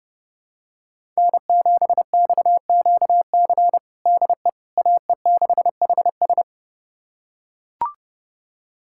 Morse code, heard as a keyed tone, about 32 wpm, 700 Hz. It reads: N7XQC DE AE6HS
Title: USA Callsign DE Callsign - 5 Characters - 30wpm